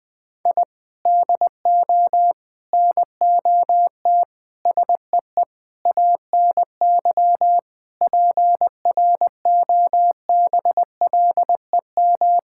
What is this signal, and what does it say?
Morse code, heard as a keyed tone, about 20 wpm, 700 Hz. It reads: I DO NOT SEE ANY PROBLEM
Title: Sentences with 5-Word Phrases - 20wpm